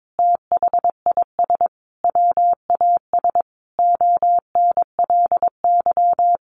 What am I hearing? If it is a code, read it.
Morse code, heard as a keyed tone, about 22 wpm, 700 Hz. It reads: THIS WAS ONLY
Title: Sentences from Top 1300 Words 22wpm